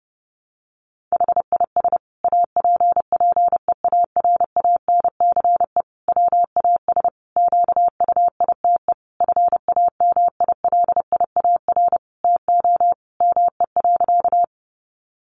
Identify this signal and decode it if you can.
Morse code, heard as a keyed tone, about 30 wpm, 700 Hz. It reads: HIS APPEARANCE WAS QUITE FAMILIAR TO ME.
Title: Sentences from Top 1100 Words 30wpm